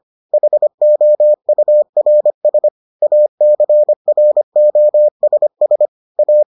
Morse code, heard as a keyed tone, about 25 wpm, 600 Hz. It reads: HOURS ACROSS A